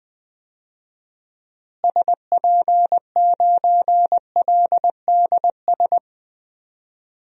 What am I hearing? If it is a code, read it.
Morse code, heard as a keyed tone, about 20 wpm, 700 Hz. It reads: SP9LDS
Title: DXCC-100 Call Signs - 6 Characters - Repeated 2x - 20wpm